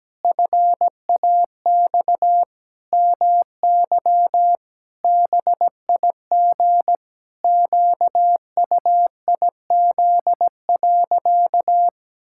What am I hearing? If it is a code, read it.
Morse code, heard as a keyed tone, about 17 wpm, 700 Hz. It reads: FAX MY BIG QUIZ.
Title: Pangram Challenge v2 - 17wpm